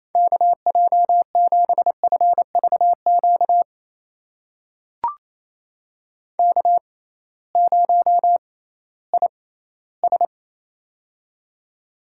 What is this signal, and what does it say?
Morse code, heard as a keyed tone, about 28 wpm, spaced as for 16 wpm, 700 Hz. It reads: KJ7FVQ K0IS
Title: USA Call Signs - 6x Character Spacing - 28wpm